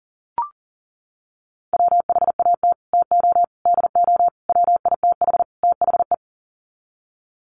Morse code, heard as a keyed tone, about 37 wpm, 700 Hz. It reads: WHAT TO DO WITH THE
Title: Sets of 5 Words Encore2 40wpm